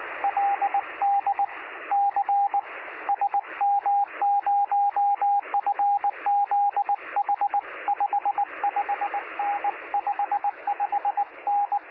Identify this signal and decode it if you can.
Morse code, heard as a keyed tone, about 20 wpm, 810 Hz, with radio noise. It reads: LD C SM0FZH 55N 55N
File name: sm7lcb / 2057z sm0fzh 23cm